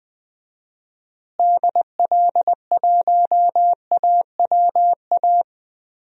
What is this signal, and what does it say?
Morse code, heard as a keyed tone, about 21 wpm, 700 Hz. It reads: DL1AWA
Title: DXCC-100 Call Signs - 6 Characters - Repeated 3x - 20wpm